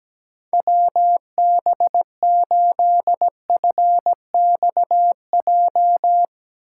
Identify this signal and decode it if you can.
Morse code, heard as a keyed tone, about 17 wpm, 700 Hz. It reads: WB8FXJ
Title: K1USN SST QSO Practice - 17wpm